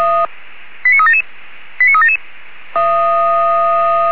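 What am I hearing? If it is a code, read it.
Five-tone selective call (ZVEI), heard as quick runs of five tones, tones of about 70 ms. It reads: 89377 89377
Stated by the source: ZVEI five-tone selective call 89376 received by SDR.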